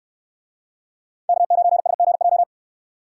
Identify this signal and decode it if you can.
Morse code, heard as a keyed tone, about 65 wpm, 690 Hz. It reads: K9SGJ